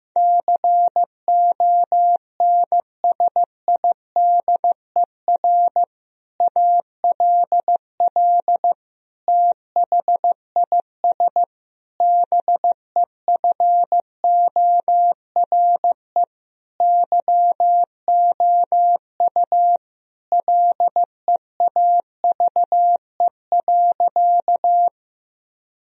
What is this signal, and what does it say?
Morse code, heard as a keyed tone, about 15 wpm, 700 Hz. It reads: CONSIDER ALL THIS BEFORE YOU LEAVE.
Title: Sentences from Top 1000 Words - Review Part 4 - 15wpm